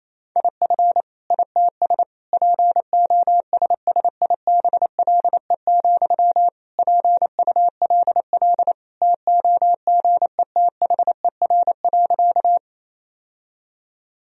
Morse code, heard as a keyed tone, about 28 wpm, 700 Hz. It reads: IF ITS POSSIBLE, PULL TOGETHER.
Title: Sentences from Top 1000 Words - Review Part 2 - 28wpm